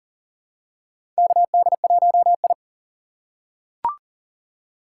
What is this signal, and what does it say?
Morse code, heard as a keyed tone, about 40 wpm, 700 Hz. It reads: KD1I
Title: USA Call Signs - 3x Character Spacing - 40wpm